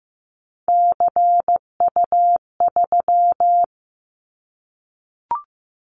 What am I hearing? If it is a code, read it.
Morse code, heard as a keyed tone, about 15 wpm, 700 Hz. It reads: CU3
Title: Callsign Prefixes for Top 100 DXCC - 15wpm